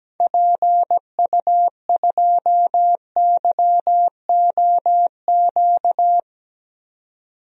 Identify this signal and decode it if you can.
Morse code, heard as a keyed tone, about 17 wpm, 700 Hz. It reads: PU2YOQ
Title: DXCC-100 Call Signs - 6 Characters - 17wpm